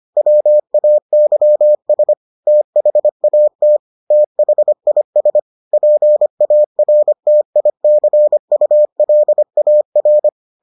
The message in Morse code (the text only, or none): WAYS THAT THIS PARTICULAR